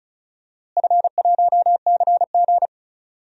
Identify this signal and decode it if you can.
Morse code, heard as a keyed tone, about 36 wpm, 700 Hz. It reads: F1CG
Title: DXCC-100 Call Signs - 4 Characters - Repeated 2x - 35wpm